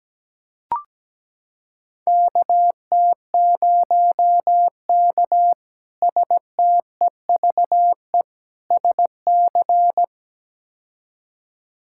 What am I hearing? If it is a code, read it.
Morse code, heard as a keyed tone, about 17 wpm, 700 Hz. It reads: KT0K STEVE SC
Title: K1USN SST QSO Practice - 17wpm